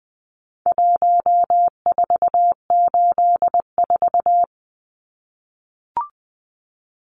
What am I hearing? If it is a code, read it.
Morse code, heard as a keyed tone, about 20 wpm, 700 Hz. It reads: 1484
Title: Numbers - Contest Style - 4-Digit 20wpm